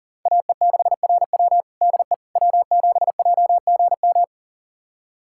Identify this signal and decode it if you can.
Morse code, heard as a keyed tone, about 37 wpm, 700 Hz. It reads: AE6RW DE W7JGM
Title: USA Callsign DE Callsign - 5 Characters - 40wpm